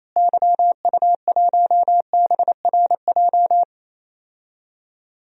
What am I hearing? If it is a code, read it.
Morse code, heard as a keyed tone, about 27 wpm, 700 Hz. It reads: YU1BRJ
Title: DXCC-100 Call Signs - 6 Characters - Repeated 2x - 28wpm